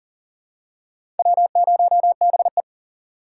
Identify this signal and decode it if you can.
Morse code, heard as a keyed tone, about 40 wpm, 700 Hz. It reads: W0BE
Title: USA Call Signs - 4 Characters - 6x Character Spacing - 40wpm